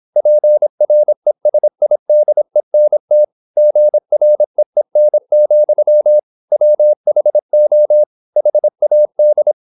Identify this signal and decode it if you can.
Morse code, heard as a keyed tone, about 26 wpm, 600 Hz. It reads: PRESIDENT GREEN, WHO HAD
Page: Never 26wpm | CW med Gnister